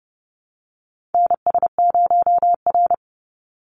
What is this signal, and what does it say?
Morse code, heard as a keyed tone, about 30 wpm, 700 Hz. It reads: NS0R
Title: Call Signs Repeated 2x - 30wpm